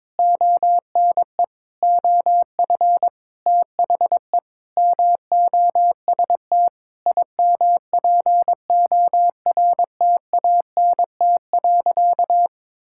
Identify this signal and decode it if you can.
Morse code, heard as a keyed tone, about 22 wpm, 700 Hz. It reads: ONE OF THE MOST IMPORTANT.
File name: practice / 22wpm / Sentences with 8-Word Phrases - 22wpm